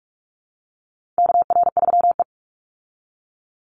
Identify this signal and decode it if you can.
Morse code, heard as a keyed tone, about 44 wpm, 700 Hz. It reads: KR3E
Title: Call Signs Repeated 2x - 45wpm